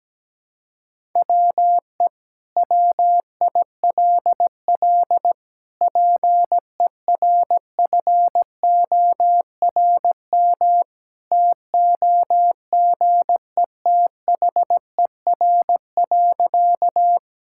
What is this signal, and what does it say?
Morse code, heard as a keyed tone, about 17 wpm, 700 Hz. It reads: WE WILL PERFORM TOGETHER.
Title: Sentences from Top 1000 Words - Review Part 5 - 17wpm